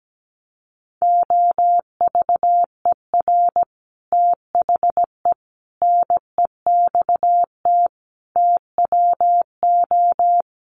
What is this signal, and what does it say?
Morse code, heard as a keyed tone, about 17 wpm, 700 Hz. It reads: OVER THE NEXT TWO
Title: Sets of 5 Words Encore2 17wpm